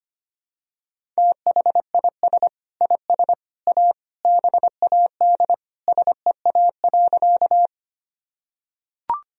Morse code, heard as a keyed tone, about 25 wpm, 700 Hz. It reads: THIS IS A BAD SEA.
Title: Sentences from Top 1000 Words - Review Part 1 - 25wpm